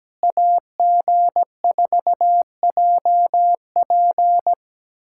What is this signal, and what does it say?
Morse code, heard as a keyed tone, about 16 wpm, 700 Hz. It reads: AG4JP
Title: SOTA QSO Practice 17wpm